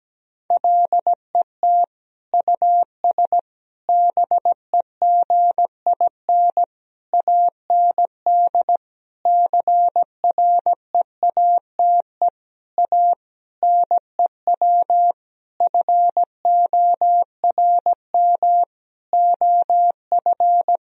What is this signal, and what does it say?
Morse code, heard as a keyed tone, about 17 wpm, 700 Hz. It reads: LET US BEGIN AND CREATE A NEW FORM OF
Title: Sentences from Top 1000 Words - Review Part 3 - 17wpm